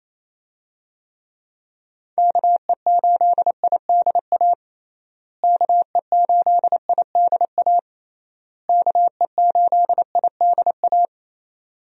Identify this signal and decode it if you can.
Morse code, heard as a keyed tone, about 28 wpm, 700 Hz. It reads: KE8IDA KE8IDA KE8IDA
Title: USA Call Signs - 6 Characters - Repeated 3x - 28wpm